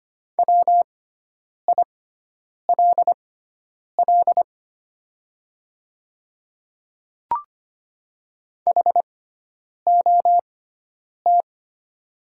Morse code, heard as a keyed tone, about 25 wpm, spaced as for 9 wpm, 690 Hz. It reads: WILL HOT
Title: 6x Character Spacing - Top 200 Words - 25wpm